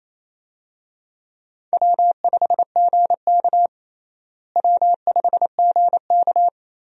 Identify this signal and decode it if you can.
Morse code, heard as a keyed tone, about 28 wpm, 700 Hz. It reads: W5GK W5GK
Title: USA Call Signs - 4 Characters - Repeated 3x - 28wpm